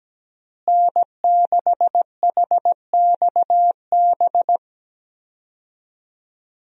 Morse code, heard as a keyed tone, about 17 wpm, 700 Hz. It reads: N6HXB
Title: USA Call Signs - 5 Characters - Repeated 3x - 17wpm